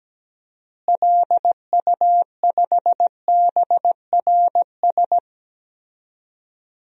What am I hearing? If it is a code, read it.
Morse code, heard as a keyed tone, about 17 wpm, 700 Hz. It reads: LU5BRS
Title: DXCC-100 Call Signs - 6 Characters - Repeated 3x - 17wpm